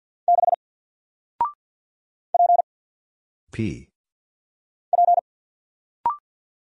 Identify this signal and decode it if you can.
Morse code, heard as a keyed tone, about 49 wpm, 700 Hz. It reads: 6 P P
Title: Single Character (Letter:Number:Punctuation:Prosign) Alt 50wpm